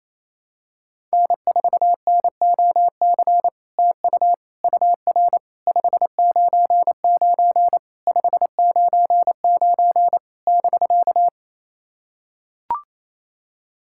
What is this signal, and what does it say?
Morse code, heard as a keyed tone, about 28 wpm, 700 Hz. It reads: N4NOC TU UR 599 599 <BK>
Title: SOTA QSO Practice 28wpm